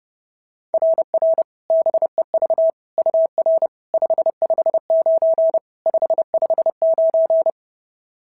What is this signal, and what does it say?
Morse code, heard as a keyed tone, about 31 wpm, 690 Hz. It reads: RR BEV UR 559 559
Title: SOTA QSO Practice 30wpm